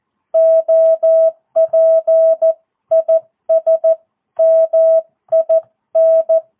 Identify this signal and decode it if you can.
Morse code, heard as a keyed tone, about 14 wpm, 650 Hz. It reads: OP IS MIN
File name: jh8cbh / 2022cw